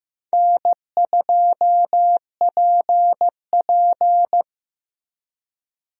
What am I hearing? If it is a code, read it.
Morse code, heard as a keyed tone, about 16 wpm, 700 Hz. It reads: N2PP
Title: USA Call Signs - 4 Characters - Repeated 2x - 15wpm